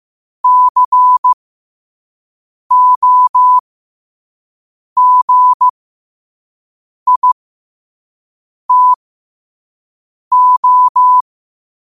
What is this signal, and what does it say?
Morse code, heard as a keyed tone, about 16 wpm, spaced as for 5 wpm, 1000 Hz. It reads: COGITO
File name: qotdfiles / 5wpm